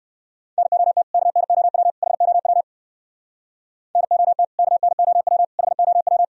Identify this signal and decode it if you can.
Morse code, heard as a keyed tone, about 67 wpm, 700 Hz. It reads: NOT KNOW HOW NOT KNOW HOW